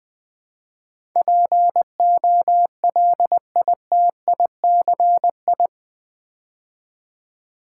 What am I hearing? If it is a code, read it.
Morse code, heard as a keyed tone, about 20 wpm, 700 Hz. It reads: POLITICI
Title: Build A Word - 4 Letters At A Time - 10-Letter Words - 20wpm